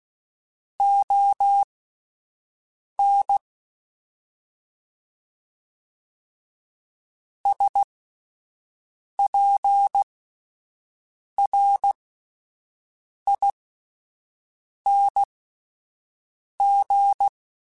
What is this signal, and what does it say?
Morse code, heard as a keyed tone, about 16 wpm, spaced as for 5 wpm, 800 Hz. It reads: ON SPRING
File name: (spaced) / QSOs Spaced-mp3s / WA7QOX 1 @10 sp